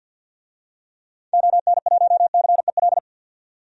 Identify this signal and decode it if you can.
Morse code, heard as a keyed tone, about 49 wpm, 700 Hz. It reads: ON1CEL
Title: DXCC-100 Call Signs - 6 Characters - 50wpm